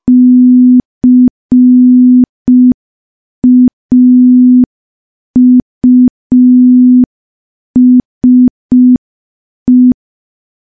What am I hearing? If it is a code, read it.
Morse code, heard as a keyed tone, about 5 wpm, 250 Hz. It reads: CAUSE
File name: morse / morse-training-05WPM